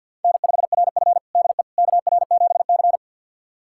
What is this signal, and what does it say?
Morse code, heard as a keyed tone, about 50 wpm, 700 Hz. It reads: N5RF DE KR7C
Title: USA Callsign DE Callsign - 4 Characters - 50wpm